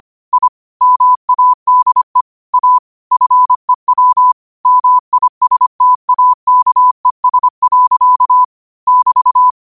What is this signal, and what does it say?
Morse code, heard as a keyed tone, about 25 wpm, 1000 Hz. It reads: I MADE A FEW MISTAKES. <BT>
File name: qotdfiles / 25wpm